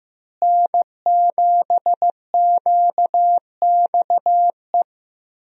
Morse code, heard as a keyed tone, about 15 wpm, 700 Hz. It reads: N7QXE
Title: USA Call Signs - 5 Characters - Repeated 2x - 15wpm